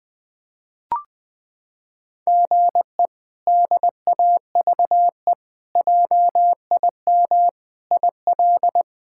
Morse code, heard as a keyed tone, about 20 wpm, 700 Hz. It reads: GE DAVE JIM IL
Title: K1USN SST QSO Practice - 20wpm